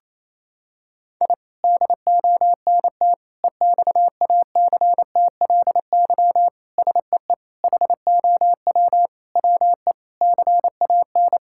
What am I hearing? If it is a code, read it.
Morse code, heard as a keyed tone, about 27 wpm, 700 Hz. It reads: I DONT EXACTLY SEE HOW WE CAN
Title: Sentences from Top 1000 Words - Review Part 6 - 28wpm